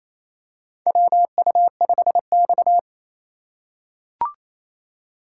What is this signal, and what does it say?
Morse code, heard as a keyed tone, about 28 wpm, 700 Hz. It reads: WU5X
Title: USA Call Signs - 4 Characters - 2x Character Spacing - 28wpm